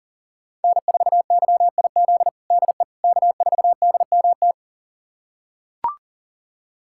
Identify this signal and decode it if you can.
Morse code, heard as a keyed tone, about 40 wpm, 700 Hz. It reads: N4YIZ DE K4DMT
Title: USA Callsign DE Callsign - 5 Characters - 40wpm